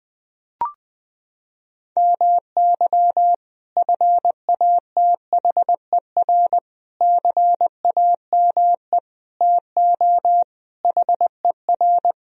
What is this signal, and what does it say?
Morse code, heard as a keyed tone, about 20 wpm, 700 Hz. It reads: MY FATHER CAME TO HER
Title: Sentences from Top 1000 Words - Review Part 1 - 20wpm